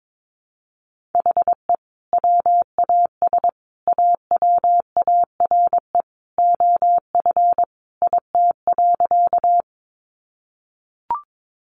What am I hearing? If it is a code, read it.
Morse code, heard as a keyed tone, about 22 wpm, 700 Hz. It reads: HE WAS AWARE OF IT.
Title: Sentences from Top 1200 Words 22wpm